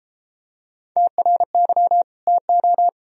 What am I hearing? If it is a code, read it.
Morse code, heard as a keyed tone, about 33 wpm, 700 Hz. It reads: TRY TO